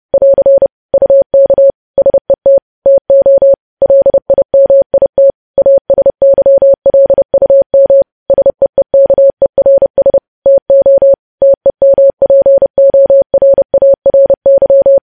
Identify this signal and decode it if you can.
Morse code, heard as a keyed tone, about 30 wpm, 550 Hz. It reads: <AR> UK SET TO LIMIT ASYLUM SEEKERS TO TEMPORARY
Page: News Headlines in Morse Code at 30 WPM